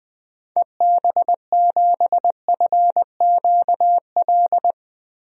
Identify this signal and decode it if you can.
Morse code, heard as a keyed tone, about 20 wpm, 700 Hz. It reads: EB7FQL
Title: DXCC-100 Call Signs - 6 Characters - 20wpm